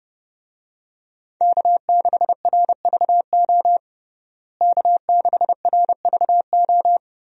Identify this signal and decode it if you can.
Morse code, heard as a keyed tone, about 32 wpm, 700 Hz. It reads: K6RVO K6RVO
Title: USA Call Signs - 5 Characters - Repeated 3x - 30wpm